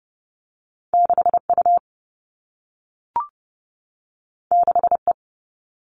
Morse code, heard as a keyed tone, about 30 wpm, 700 Hz. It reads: 6U 6E
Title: All Cut Numbers - 2-Digit 30wpm